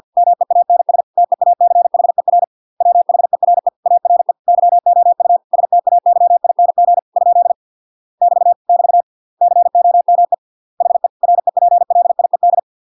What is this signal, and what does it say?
Morse code, heard as a keyed tone, about 50 wpm, 700 Hz. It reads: MEANS TEACHER WHERE ARE YOU STAYING ? <BT> <BT> COME HE REPLIED